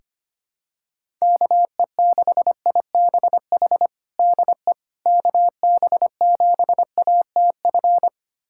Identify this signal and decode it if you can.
Morse code, heard as a keyed tone, about 25 wpm, 700 Hz. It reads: KE6IBH DE KB7ATF
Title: USA Callsign DE Callsign - 6 Characters - 25wpm